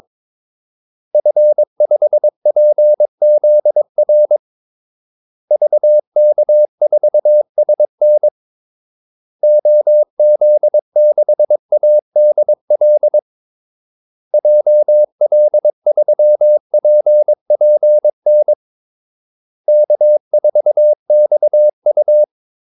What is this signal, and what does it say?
Morse code, heard as a keyed tone, about 22 wpm, 600 Hz. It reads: F5PZR VK4SN OZ6ADL JL3PPN K4XU
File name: Callsigns-22wpm